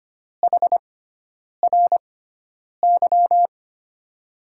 Morse code, heard as a keyed tone, about 25 wpm, spaced as for 9 wpm, 700 Hz. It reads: HRY
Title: USA Call Signs - 6 Characters - 6x Character Spacing - 25wpm